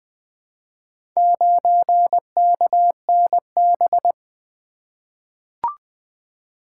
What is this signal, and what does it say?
Morse code, heard as a keyed tone, about 20 wpm, 700 Hz. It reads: 9KNB
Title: Four Letter-Number Combinations - 20wpm